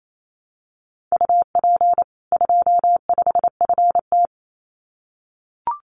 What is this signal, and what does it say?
Morse code, heard as a keyed tone, about 28 wpm, 700 Hz. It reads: UP 25FT
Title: Common QSO Phrases 28wpm